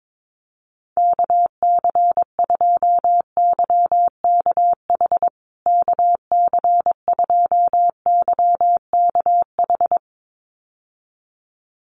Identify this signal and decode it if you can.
Morse code, heard as a keyed tone, about 22 wpm, 700 Hz. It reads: KC2YKH KC2YKH
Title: USA Call Signs Repeated 2x 22wpm